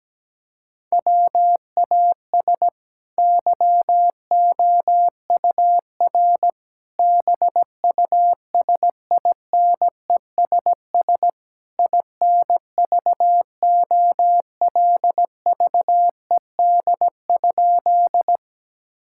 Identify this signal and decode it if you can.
Morse code, heard as a keyed tone, about 17 wpm, 700 Hz. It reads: WAS YOUR BUSINESS INVOLVED?
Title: Sentences from Top 1000 Words - Review Part 5 - 17wpm